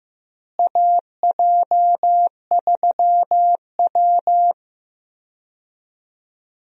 Morse code, heard as a keyed tone, about 15 wpm, 700 Hz. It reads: AJ3W
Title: USA Call Signs - 4 Characters - Repeated 2x - 15wpm